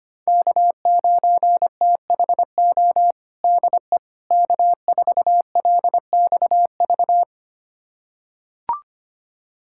Morse code, heard as a keyed tone, about 25 wpm, 700 Hz. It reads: K9THO DE K4LXV
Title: USA Callsign DE Callsign 25wpm